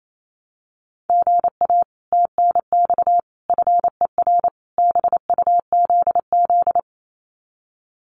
Morse code, heard as a keyed tone, about 28 wpm, 700 Hz. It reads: GA TNX FER BUZZ
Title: Common QSO Phrases Encore 28wpm